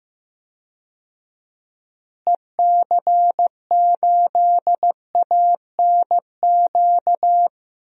Morse code, heard as a keyed tone, about 15 wpm, 700 Hz. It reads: EC8ANQ
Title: DXCC-100 Call Signs - 6 Characters - Repeated 2x - 15wpm